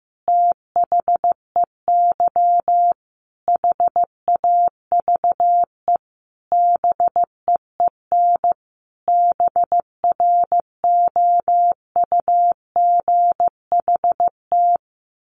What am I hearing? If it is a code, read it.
Morse code, heard as a keyed tone, about 15 wpm, 700 Hz. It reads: THEY HAVE BEEN BROUGHT
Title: Sentences from Top 1500 Words 15wpm